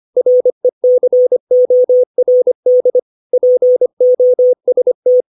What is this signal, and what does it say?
Morse code, heard as a keyed tone, about 25 wpm, 500 Hz. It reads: RECORD POST
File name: 632 words / 632 Words 25WPM